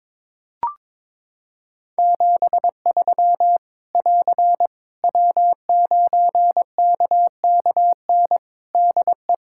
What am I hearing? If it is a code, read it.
Morse code, heard as a keyed tone, about 22 wpm, 700 Hz. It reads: 73 <AR> W9KKN DE
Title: Practice QSO - Exchange 4 - Ending - 22wpm